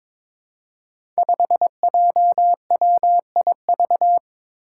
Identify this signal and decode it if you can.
Morse code, heard as a keyed tone, about 22 wpm, 700 Hz. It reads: 5JWIV
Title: Five Challenging Letter-Number Combinations 22wpm